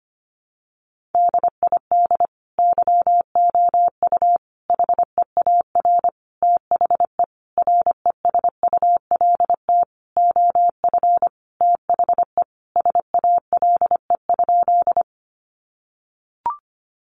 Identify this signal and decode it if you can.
Morse code, heard as a keyed tone, about 25 wpm, 700 Hz. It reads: DID YOU HEAR THE RESULT OF THE SALE?
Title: Sentences from Top 1400 Words 25wpm